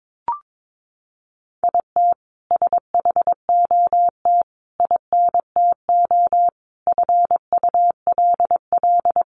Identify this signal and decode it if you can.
Morse code, heard as a keyed tone, about 22 wpm, 700 Hz. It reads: IT SHOT INTO FULL
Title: Sentences from Top 1100 Words 22wpm